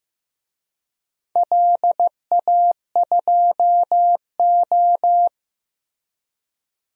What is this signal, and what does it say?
Morse code, heard as a keyed tone, about 15 wpm, 700 Hz. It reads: LA2O
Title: DXCC-100 Call Signs - 4 Characters - Repeated 3x - 15wpm